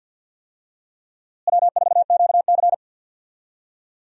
Morse code, heard as a keyed tone, about 52 wpm, 700 Hz. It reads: W4QC
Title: USA Call Signs - 4 Characters - Repeated 2x - 50wpm